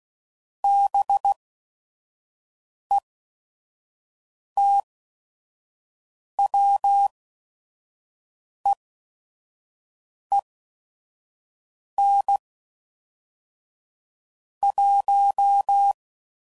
Morse code, heard as a keyed tone, about 16 wpm, spaced as for 5 wpm, 800 Hz. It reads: BETWEEN 1
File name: CW Code Groups @5WPM / Weather Sets / Weather 5 1